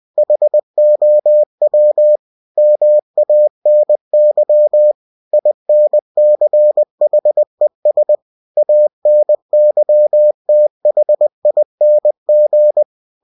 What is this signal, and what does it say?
Morse code, heard as a keyed tone, about 20 wpm, 600 Hz. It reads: HOW MANY INCHES ANYTHING
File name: cw-books / winnie-the-pooh / winnie-20wpm-page-0109